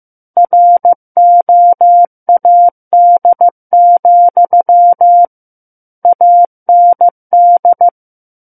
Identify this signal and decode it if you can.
Morse code, heard as a keyed tone, about 15 wpm, 700 Hz. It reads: ROAD, AND
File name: Book One / 06-Chapter 6 - 15 WPM